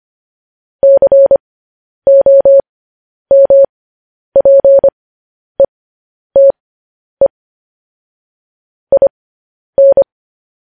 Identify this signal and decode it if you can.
Morse code, heard as a keyed tone, about 25 wpm, spaced as for 10 wpm, 550 Hz. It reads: COMPETE IN
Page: news-headlines-in-morse-code-at-10-wpm